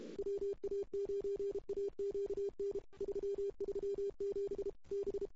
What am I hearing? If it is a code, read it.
Morse code, heard as a keyed tone, about 32 wpm, 400 Hz. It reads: WA9AQN 337 B
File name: CW / 2008-ss-cw-k5tr-op-n5rz -00:00